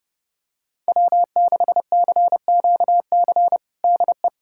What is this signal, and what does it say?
Morse code, heard as a keyed tone, about 32 wpm, 700 Hz. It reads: W6CQC DE
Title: USA Callsign DE Callsign 30wpm